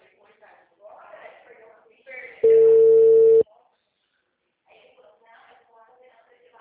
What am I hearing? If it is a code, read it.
Telephone call progress tone (China, 450 Hz), heard as a telephone network's tone, about 450 Hz, with speech. Ringback tone.